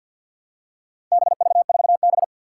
Morse code, heard as a keyed tone, about 51 wpm, 700 Hz. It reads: BV4B
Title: Four Challenging Letter-Number Combinations 50wpm